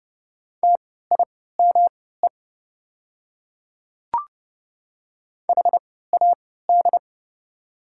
Morse code, heard as a keyed tone, about 30 wpm, spaced as for 17 wpm, 700 Hz. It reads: TIME HAD
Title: 3x Character Spacing - Top 100 Words - 30wpm